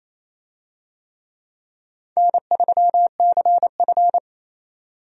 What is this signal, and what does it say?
Morse code, heard as a keyed tone, about 28 wpm, 700 Hz. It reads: N3CF
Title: USA Call Signs - 4 Characters - 28wpm